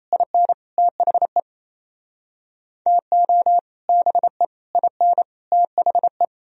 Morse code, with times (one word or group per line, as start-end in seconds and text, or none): 0.1-0.5 IN
0.8-1.4 THE
2.9-3.6 TO
3.9-4.5 BE
4.8-5.2 IN
5.5-6.2 THE